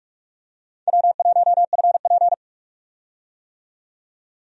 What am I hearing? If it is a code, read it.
Morse code, heard as a keyed tone, about 44 wpm, 700 Hz. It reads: W1FP
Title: USA Call Signs - 4 Characters - 45wpm